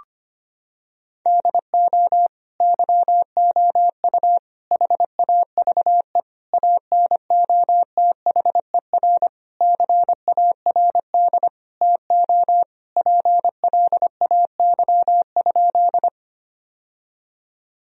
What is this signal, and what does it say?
Morse code, heard as a keyed tone, about 25 wpm, 700 Hz. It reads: DO YOU HAVE ANOTHER CARD TO PLAY?
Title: Sentences from Top 1000 Words - Review Part 4 - 25wpm